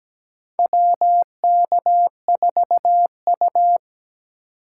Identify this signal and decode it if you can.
Morse code, heard as a keyed tone, about 17 wpm, 700 Hz. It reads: WK4U
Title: USA Call Signs - 6x Character Spacing - 17wpm